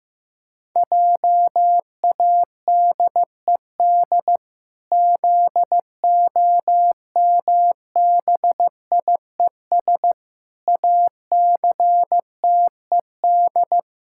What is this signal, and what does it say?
Morse code, heard as a keyed tone, about 15 wpm, 700 Hz. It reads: JADED ZOMBIES ACTED
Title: Pangram Challenge v2 - 15wpm